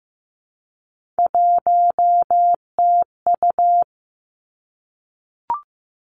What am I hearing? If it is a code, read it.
Morse code, heard as a keyed tone, about 15 wpm, 700 Hz. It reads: 1TU
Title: Three Challenging Letter-Number Combinations 15wpm